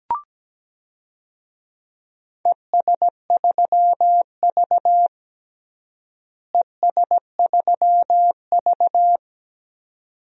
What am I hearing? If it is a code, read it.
Morse code, heard as a keyed tone, about 17 wpm, 700 Hz. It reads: ES3V ES3V
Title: DXCC-100 Call Signs - 4 Characters - Repeated 2x - 17wpm